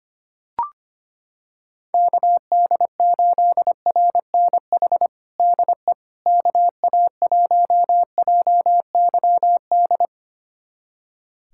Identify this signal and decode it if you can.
Morse code, heard as a keyed tone, about 25 wpm, 700 Hz. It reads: KD8RNH DE KA1JYD
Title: USA Callsign DE Callsign 25wpm